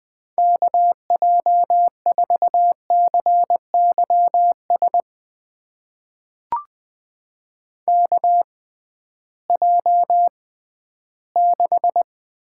Morse code, heard as a keyed tone, about 20 wpm, spaced as for 13 wpm, 700 Hz. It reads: KJ4CYS KJ6
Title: USA Call Signs - 6 Characters - 6x Character Spacing - 20wpm